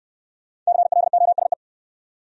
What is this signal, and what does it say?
Morse code, heard as a keyed tone, about 65 wpm, 700 Hz. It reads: CLOSE